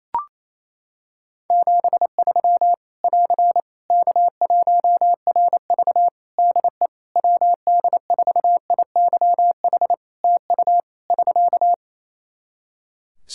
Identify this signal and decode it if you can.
Morse code, heard as a keyed tone, about 27 wpm, 700 Hz. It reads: 73 <AR> K1RV DE WD4IYH TU <SK>
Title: Practice QSO - Exchange 4 - Ending - 28wpm